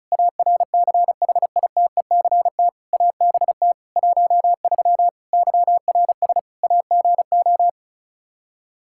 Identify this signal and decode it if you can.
Morse code, heard as a keyed tone, about 36 wpm, 700 Hz. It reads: ARCHITECT ABT 13 YRS AGO
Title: Practice QSO - Exchange 3 - More Info - 35wpm